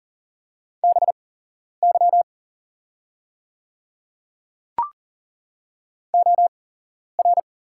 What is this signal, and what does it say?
Morse code, heard as a keyed tone, about 39 wpm, spaced as for 11 wpm, 700 Hz. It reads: BY OR
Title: 8x Character Spacing - Top 100 Words - 40wpm